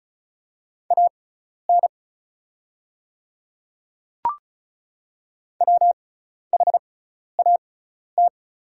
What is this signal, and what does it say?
Morse code, heard as a keyed tone, about 36 wpm, spaced as for 12 wpm, 690 Hz. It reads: AN WHAT
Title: 6x Character Spacing - Top 100 Words - 35wpm